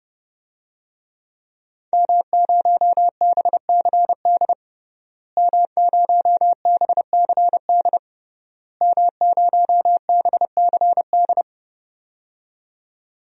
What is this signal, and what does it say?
Morse code, heard as a keyed tone, about 30 wpm, 700 Hz. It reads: M0BCD M0BCD M0BCD
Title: DXCC-100 Call Signs - 5 Characters - Repeated 3x - 30wpm